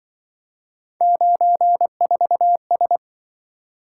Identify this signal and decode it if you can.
Morse code, heard as a keyed tone, about 24 wpm, 700 Hz. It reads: 94S
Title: Speed-Racing - Three Letter-Number Combinations - 20wpm